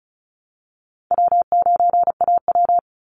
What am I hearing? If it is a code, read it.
Morse code, heard as a keyed tone, about 34 wpm, 700 Hz. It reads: W9AW
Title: Call Signs - 35wpm